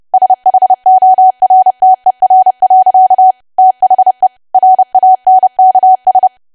Morse code, heard as a keyed tone, about 30 wpm, 750 Hz. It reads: SHORTER. THE RANKS
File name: cwa / W1AW-Level3 / 30WPM